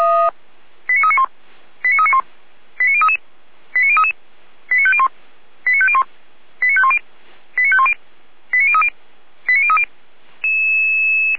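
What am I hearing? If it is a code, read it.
Five-tone selective call (ZVEI), heard as quick runs of five tones, tones of about 80 ms. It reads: 89391 89391 89033 89033 89501 89501 89510 89510 89030 89030
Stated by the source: ZVEI five-tone selective call 89042 received on an SDR.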